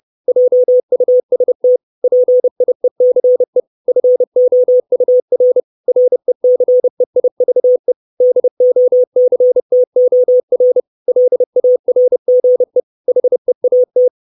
Morse code, heard as a keyed tone, about 30 wpm, 500 Hz. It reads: JUST PIECE FOUR RECEIVE DOCTOR LARGE HEAT
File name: 632 Words 30WPM